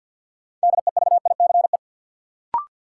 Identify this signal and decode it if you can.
Morse code, heard as a keyed tone, about 51 wpm, 700 Hz. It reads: DEVICE
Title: Sentences from Top 1500 Words - New Word Drill Alt - 50wpm